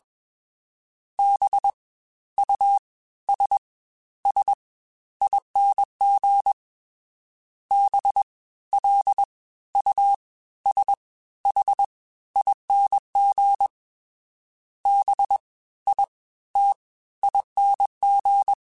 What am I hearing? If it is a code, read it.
Morse code, heard as a keyed tone, about 21 wpm, 800 Hz. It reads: B U S S ING B L U S H ING B I T ING
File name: Special INGs-mp3s / Special INGs-mp3s / Fast ing @20